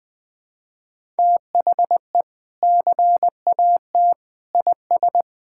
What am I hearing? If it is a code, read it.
Morse code, heard as a keyed tone, about 20 wpm, 700 Hz. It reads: THE CAT IS
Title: Sentences with 2-Word Phrases - Spoken Beginning - 20wpm